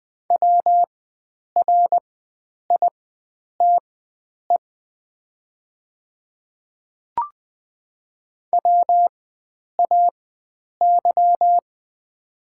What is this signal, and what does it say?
Morse code, heard as a keyed tone, about 20 wpm, spaced as for 9 wpm, 700 Hz. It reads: WRITE WAY
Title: 4x Character Spacing - Top 100 Words - 20wpm